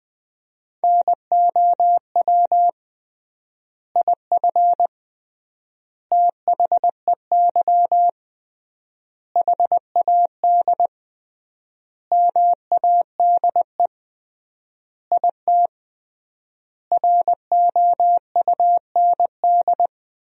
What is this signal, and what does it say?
Morse code, heard as a keyed tone, about 20 wpm, 700 Hz. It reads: NOW IF THEY HAD MADE IT ROUND
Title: 3x Word-Spacing - Sentences from Top 500 Words - 20wpm